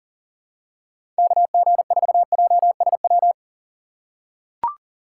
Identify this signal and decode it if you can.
Morse code, heard as a keyed tone, about 40 wpm, 700 Hz. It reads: KG4JSW
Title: USA Call Signs - 6 Characters - 4x Character Spacing - 40wpm